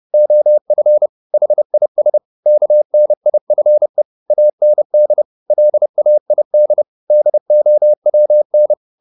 Morse code, heard as a keyed tone, about 30 wpm, 600 Hz. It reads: OF HIS KNIFE AND LAID DOWN